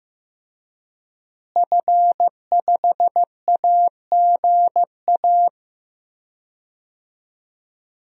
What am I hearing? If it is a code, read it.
Morse code, heard as a keyed tone, about 16 wpm, 700 Hz. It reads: F5AGA
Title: DXCC-100 Call Signs - 5 Characters - Repeated 2x - 15wpm